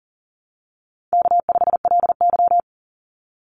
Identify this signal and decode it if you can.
Morse code, heard as a keyed tone, about 40 wpm, 700 Hz. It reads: K5LY
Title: Easy Call Signs Repeated 3x - 40wpm